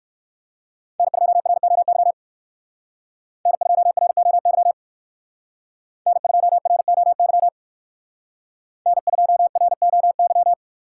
Morse code, heard as a keyed tone, about 58 wpm, 700 Hz. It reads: N2ROY N2ROY N2ROY N2ROY